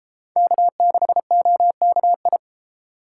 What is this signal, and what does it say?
Morse code, heard as a keyed tone, about 29 wpm, 700 Hz. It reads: K6OKI